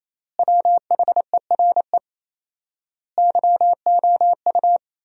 Morse code, heard as a keyed tone, about 28 wpm, 700 Hz. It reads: WHERE YOU
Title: 4x Word-Spacing - Sentences from Top 200 Words - 28wpm